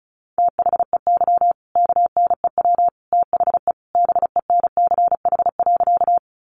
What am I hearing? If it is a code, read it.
Morse code, heard as a keyed tone, about 34 wpm, 700 Hz. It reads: THEY KNEW THE BENCH.
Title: Sentences from Top 1400 Words 35wpm